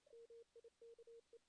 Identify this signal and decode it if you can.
Morse code, heard as a keyed tone, about 34 wpm, 440 Hz. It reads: MIKE
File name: 2016 JAN CW NAQP / 2016 Jan naqp k5ot - 0-2